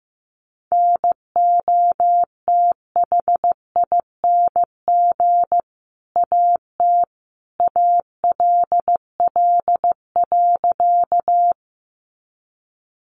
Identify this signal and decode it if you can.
Morse code, heard as a keyed tone, about 15 wpm, 700 Hz. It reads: NOTHING AT ALL.
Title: Sentences from Top 500 Words 15wpm